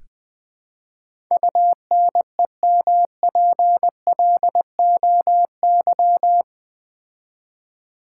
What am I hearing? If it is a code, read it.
Morse code, heard as a keyed tone, about 20 wpm, 700 Hz. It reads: UNEMPLOY
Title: Build A Word - 4 Letters At A Time - 12-Letter Words - 20wpm